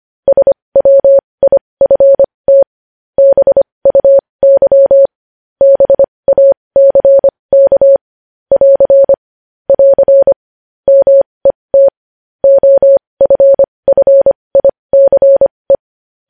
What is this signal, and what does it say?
Morse code, heard as a keyed tone, about 25 wpm, spaced as for 20 wpm, 550 Hz. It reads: SWIFT BUY BACK <AR> <AR> MET OFFICE
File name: podcast / News headlines-1x-25x20